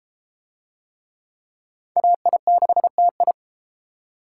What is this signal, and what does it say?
Morse code, heard as a keyed tone, about 30 wpm, 700 Hz. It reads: AI6TI